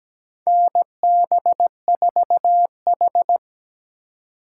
Morse code, heard as a keyed tone, about 17 wpm, 700 Hz. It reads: NB4H
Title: USA Call Signs - 4 Characters - 4x Character Spacing - 17wpm